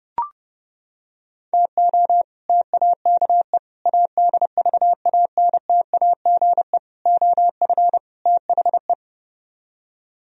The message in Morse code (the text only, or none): TO TAKE ADVANTAGE OF THE